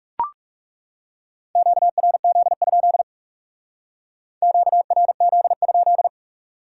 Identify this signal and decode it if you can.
Morse code, heard as a keyed tone, about 36 wpm, 700 Hz. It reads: QRZ? QRZ?
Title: Speed-Racing Top 90 QSO Elements 30wpm